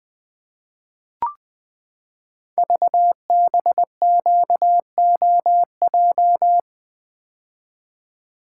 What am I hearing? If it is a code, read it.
Morse code, heard as a keyed tone, about 21 wpm, 700 Hz. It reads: VBQOJ
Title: Code Groups - 5 Letters - 20wpm